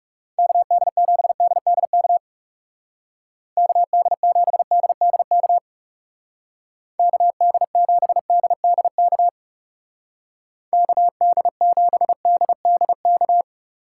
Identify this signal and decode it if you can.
Morse code, heard as a keyed tone, about 40 wpm, 700 Hz. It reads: KD7DDK KD7DDK KD7DDK KD7DDK